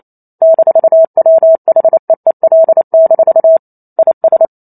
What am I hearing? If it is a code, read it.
Morse code, heard as a keyed tone, about 28 wpm, 650 Hz. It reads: -WHEEL- IS
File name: Chapter 2 - 28 WPM